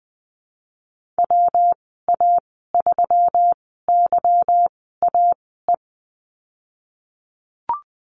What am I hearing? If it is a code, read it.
Morse code, heard as a keyed tone, about 19 wpm, 700 Hz. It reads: W A 3 Y A E
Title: 238 Lesson 38g - Letter q - Call Signs 20wpm